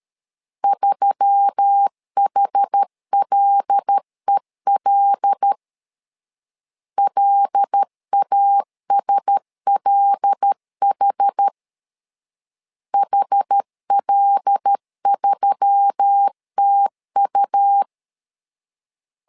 Morse code, heard as a keyed tone, about 13 wpm, 800 Hz. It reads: 3HLEL LASLH HL3TU
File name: LEZIONI / 5 LBVC / 5 L 12wpm 1913